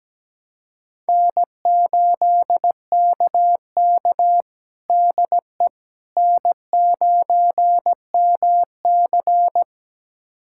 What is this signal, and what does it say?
Morse code, heard as a keyed tone, about 17 wpm, 700 Hz. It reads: N8KK DE N9MC
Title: USA Callsign DE Callsign - 4 Characters - 17wpm